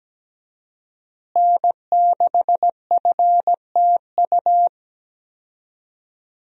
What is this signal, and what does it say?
Morse code, heard as a keyed tone, about 17 wpm, 700 Hz. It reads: N6FTU
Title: USA Call Signs - 5 Characters - Repeated 3x - 17wpm